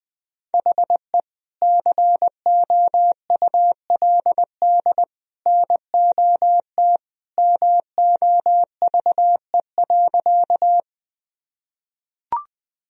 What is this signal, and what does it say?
Morse code, heard as a keyed tone, about 21 wpm, 700 Hz. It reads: HE COULD NOT MOVE.
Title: Sentences from Top 1000 Words - Review Part 1 - 20wpm